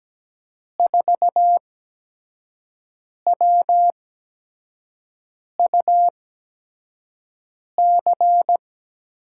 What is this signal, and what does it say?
Morse code, heard as a keyed tone, about 17 wpm, spaced as for 5 wpm, 700 Hz. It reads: 4WUC
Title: USA Call Signs - 8x Character Spacing - 17wpm